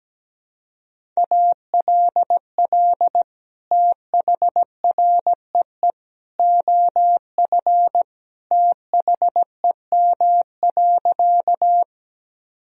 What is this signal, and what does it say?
Morse code, heard as a keyed tone, about 17 wpm, 700 Hz. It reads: ALL THREE OF THEM.
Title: Sentences from Top 1000 Words - Review Part 1 - 17wpm